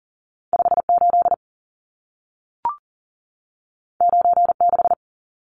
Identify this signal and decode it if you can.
Morse code, heard as a keyed tone, about 40 wpm, 700 Hz. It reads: 58 96
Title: Numbers - 2-Digit 40wpm